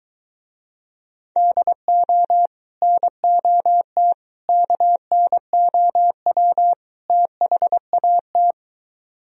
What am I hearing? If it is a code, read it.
Morse code, heard as a keyed tone, about 23 wpm, 700 Hz. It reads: DO NOT KNOW THAT